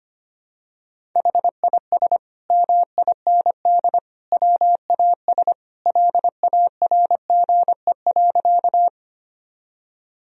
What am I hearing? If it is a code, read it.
Morse code, heard as a keyed tone, about 25 wpm, 700 Hz. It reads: HIS MIND WAS LARGE.
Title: Sentences from Top 1000 Words - Review Part 1 - 25wpm